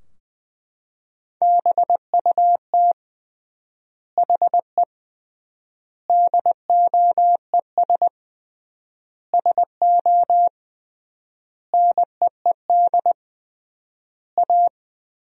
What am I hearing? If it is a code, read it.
Morse code, heard as a keyed tone, about 20 wpm, 700 Hz. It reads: BUT HE DOES SO NEED A
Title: 3x Word-Spacing - Sentences from Top 200 Words - 20wpm